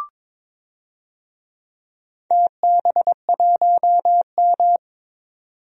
Morse code, heard as a keyed tone, about 22 wpm, 700 Hz. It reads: TB1M
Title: DXCC-100 Call Signs - 4 Characters - 22wpm